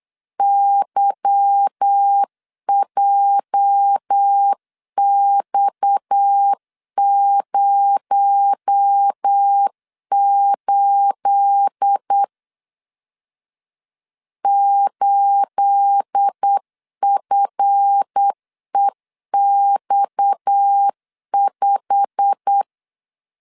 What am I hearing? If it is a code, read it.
Morse code, heard as a keyed tone, about 8 wpm, 800 Hz. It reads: YJX08 8FEX5
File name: LEZIONI / 9 28 8wpm 3512